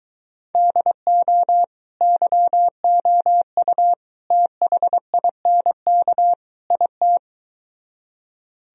Morse code, heard as a keyed tone, about 23 wpm, 700 Hz. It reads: DO YOU THINK IT